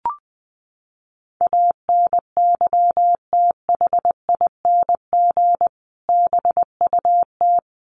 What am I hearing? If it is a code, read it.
Morse code, heard as a keyed tone, about 20 wpm, 700 Hz. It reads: ANYTHING BUT
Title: Sentences from Top 1000 Words 20wpm